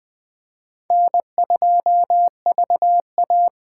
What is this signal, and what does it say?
Morse code, heard as a keyed tone, about 20 wpm, 700 Hz. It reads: N2VA
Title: Speed Racing - USA 1x2 Call Signs 17wpm